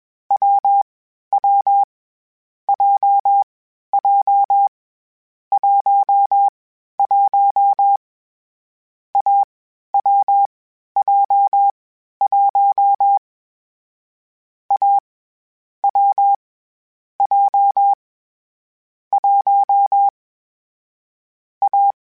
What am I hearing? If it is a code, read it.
Morse code, heard as a keyed tone, about 21 wpm, 800 Hz. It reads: W W J J 1 1 A W J 1 A W J 1 A
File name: Difficult Combos / Difficult Combos-mp3s / AWJ1 @20